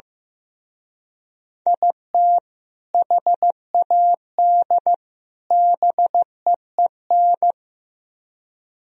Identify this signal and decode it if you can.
Morse code, heard as a keyed tone, about 17 wpm, 700 Hz. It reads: IT HAD BEEN
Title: Speed-Racing - Sets of 3 Words 15wpm